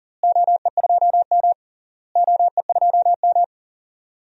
Morse code, heard as a keyed tone, about 40 wpm, 700 Hz. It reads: OE2M OE2M
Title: DXCC-100 Call Signs - 4 Characters - Repeated 3x - 40wpm